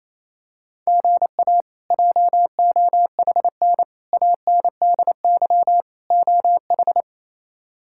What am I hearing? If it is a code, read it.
Morse code, heard as a keyed tone, about 27 wpm, 700 Hz. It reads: GA JOHN ANDY OH
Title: K1USN SST QSO Practice - 28wpm